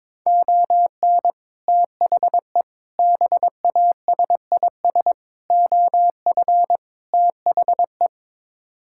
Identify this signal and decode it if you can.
Morse code, heard as a keyed tone, about 22 wpm, 700 Hz. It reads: ON THE BASIS OF THE
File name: practice / 22wpm / Build A Phrase - 5-Word Phrases - 22wpm